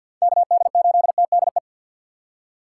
Morse code, heard as a keyed tone, about 53 wpm, 700 Hz. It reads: KD8TDE
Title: USA Call Signs - 6 Characters - 3x Character Spacing - 50wpm